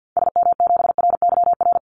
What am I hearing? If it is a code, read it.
Morse code, heard as a keyed tone, about 50 wpm, 700 Hz. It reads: SM7RYR
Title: Call Signs Repeated 2x - 50wpm